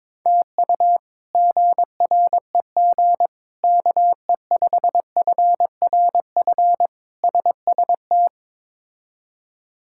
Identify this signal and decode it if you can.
Morse code, heard as a keyed tone, about 22 wpm, 700 Hz. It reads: TU GREG KE5FRF SST
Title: K1USN SST QSO Practice - 22wpm